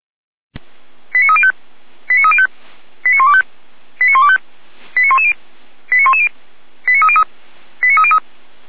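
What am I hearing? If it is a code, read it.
Five-tone selective call (ZVEI), heard as quick runs of five tones, tones of about 80 ms. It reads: 89305 89305 89125 89125 89110 89110 89303 89303
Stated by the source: ZVEI five-tone selective call received by SDR, logged by feuerwehr-orlamuende.org (Feuerwehr Orlamünde fire service).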